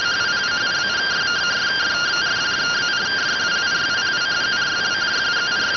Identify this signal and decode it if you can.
Two-tone teleprinter signal, about 170 Hz shift.